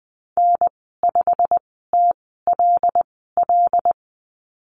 Morse code, heard as a keyed tone, about 20 wpm, spaced as for 15 wpm, 700 Hz. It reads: N5TLL
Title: 122 Lesson 20g - Number 3 - Call Signs 20wpm